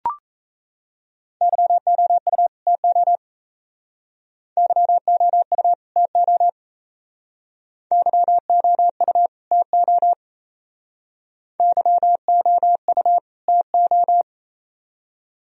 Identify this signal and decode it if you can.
Morse code, heard as a keyed tone, about 35 wpm, 700 Hz. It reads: YOU TO YOU TO YOU TO YOU TO